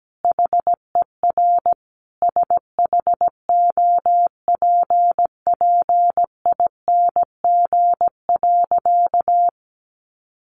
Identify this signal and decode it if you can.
Morse code, heard as a keyed tone, about 17 wpm, 700 Hz. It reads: HER SHOPPING.
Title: Additional Easy Phrases 17wpm